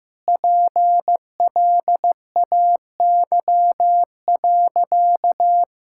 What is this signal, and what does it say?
Morse code, heard as a keyed tone, about 15 wpm, 700 Hz. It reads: PLAY.
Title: Sentences from Top 1000 Words - Review Part 6 - 15wpm